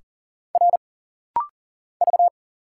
Morse code, heard as a keyed tone, about 40 wpm, 700 Hz. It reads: R V
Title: Single Letter-Number Alt - Rapid-Fire 40wpm